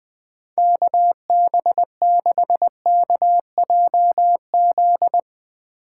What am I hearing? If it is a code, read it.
Morse code, heard as a keyed tone, about 20 wpm, 700 Hz. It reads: KB6KJZ
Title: Speed Racing - USA 2x3 Call Signs 17wpm